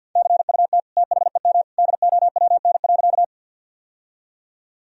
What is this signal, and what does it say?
Morse code, heard as a keyed tone, about 52 wpm, 700 Hz. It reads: CUT THEM DOWN.
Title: Sentences from Top 1000 Words - Review Part 1 - 50wpm